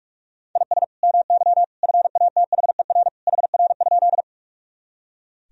Morse code, heard as a keyed tone, about 46 wpm, 700 Hz. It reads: IS MY FATHER HR?
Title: Sentences from Top 300 Words and Abbreviations 45wpm